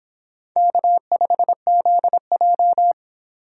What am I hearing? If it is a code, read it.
Morse code, heard as a keyed tone, about 25 wpm, 700 Hz. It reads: K5ZJ